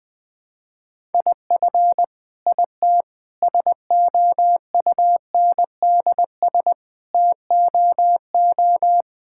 Morse code, heard as a keyed tone, about 21 wpm, 700 Hz. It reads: IF IT SOUNDS TOO
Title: Sentences with 9-Word Phrases - Spoken Beginning - 20wpm